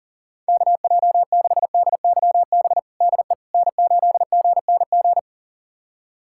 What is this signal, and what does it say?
Morse code, heard as a keyed tone, about 40 wpm, 700 Hz. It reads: KJ6DYB DE N8GNG
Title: USA Callsign DE Callsign 40wpm